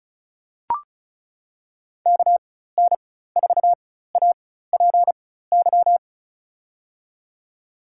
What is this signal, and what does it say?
Morse code, heard as a keyed tone, about 36 wpm, spaced as for 16 wpm, 700 Hz. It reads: KN4APY
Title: USA Call Signs - 6 Characters - 4x Character Spacing - 35wpm